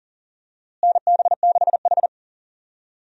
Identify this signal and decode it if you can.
Morse code, heard as a keyed tone, about 40 wpm, 700 Hz. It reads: NB6H
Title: USA Call Signs - 4 Characters - Repeated 3x - 40wpm